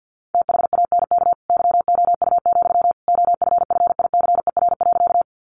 Morse code, heard as a keyed tone, about 50 wpm, 700 Hz. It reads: THANK YOU, OFFICER.
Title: Sentences from Top 1200 Words 50wpm